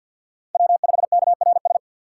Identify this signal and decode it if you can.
Morse code, heard as a keyed tone, about 51 wpm, 700 Hz. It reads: W5KRS
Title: USA Call Signs - 5 Characters - 6x Character Spacing - 50wpm